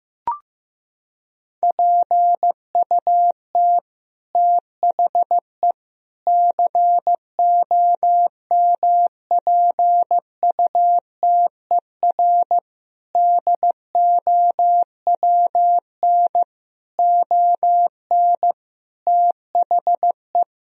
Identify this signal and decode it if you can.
Morse code, heard as a keyed tone, about 15 wpm, 700 Hz. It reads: PUT THE COMPUTER DOWN ON THE
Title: Sentences from Top 1000 Words - Review Part 3 - 15wpm